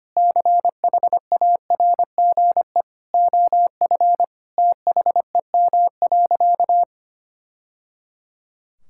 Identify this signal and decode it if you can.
Morse code, heard as a keyed tone, about 25 wpm, 700 Hz. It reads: CHARGE OF THEM.
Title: Sentences from Top 1000 Words - Review Part 6 - 25wpm